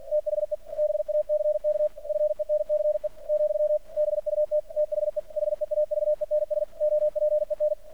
Morse code, heard as a keyed tone, about 48 wpm, 600 Hz. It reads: THE BACK FENCE , BUT THE HEAVENS OPEN
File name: archived / CRSnet